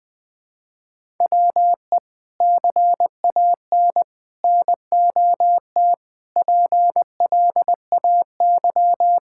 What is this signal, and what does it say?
Morse code, heard as a keyed tone, about 20 wpm, 700 Hz. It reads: WE CAN NOT PLAY
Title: Sentences from Top 1000 Words - Review Part 6 - 20wpm